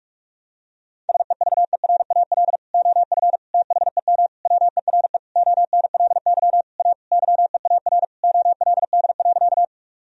Morse code, heard as a keyed tone, about 46 wpm, 700 Hz. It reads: SEVERAL OF THEM WERE ONLY A YEAR OLD.
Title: Sentences from Top 1000 Words - Review Part 2 - 45wpm